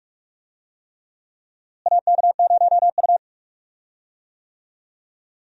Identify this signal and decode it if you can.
Morse code, heard as a keyed tone, about 46 wpm, 700 Hz. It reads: AK0U
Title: USA Call Signs - 4 Characters - Repeated 3x - 45wpm